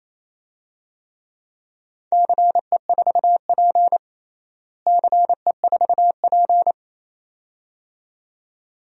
Morse code, heard as a keyed tone, about 28 wpm, 700 Hz. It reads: CE4P CE4P
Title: DXCC-100 Call Signs - 4 Characters - Repeated 2x - 28wpm